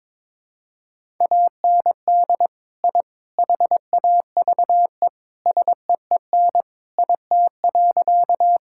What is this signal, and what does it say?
Morse code, heard as a keyed tone, about 22 wpm, 700 Hz. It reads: AND I HAVE SEEN IT.
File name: practice / 22wpm / Sentences with 3-Word Phrases - 22wpm